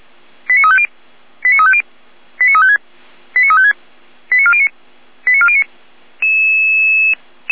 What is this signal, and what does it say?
Five-tone selective call (ZVEI), heard as quick runs of five tones, tones of about 70 ms. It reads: 89350 89350 89356 89356 89409 89409
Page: ZVEI